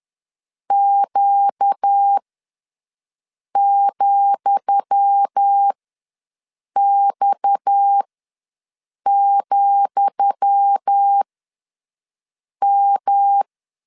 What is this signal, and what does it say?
Morse code, heard as a keyed tone, about 10 wpm, spaced as for 5 wpm, 800 Hz. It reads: Q,X,M
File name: LEZIONI / 10 SIGNS / 10D COMA ws 10wpm 4314